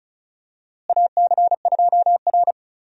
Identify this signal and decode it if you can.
Morse code, heard as a keyed tone, about 34 wpm, 700 Hz. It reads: AC2R
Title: USA Call Signs - 4 Characters - Repeated 2x - 35wpm